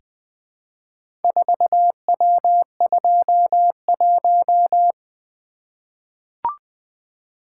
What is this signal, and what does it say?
Morse code, heard as a keyed tone, about 20 wpm, 700 Hz. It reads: 4W21
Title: Four Letter-Number Combinations - 20wpm